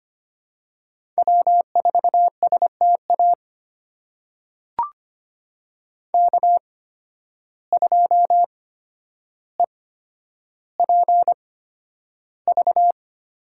Morse code, heard as a keyed tone, about 25 wpm, spaced as for 11 wpm, 700 Hz. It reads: W4STA K2EPV
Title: USA Call Signs - 5 Characters - 8x Character Spacing - 25wpm